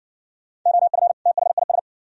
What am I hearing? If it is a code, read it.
Morse code, heard as a keyed tone, about 60 wpm, 700 Hz. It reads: OF THIS